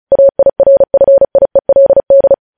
Morse code, heard as a keyed tone, about 36 wpm, 550 Hz. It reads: AIRFIELD
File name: podcast / News headlines-1x-35x35